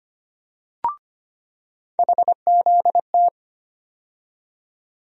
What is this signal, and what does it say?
Morse code, heard as a keyed tone, about 25 wpm, 700 Hz. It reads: HZT
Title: USA Call Sign Postfixes - 3 Letters 25wpm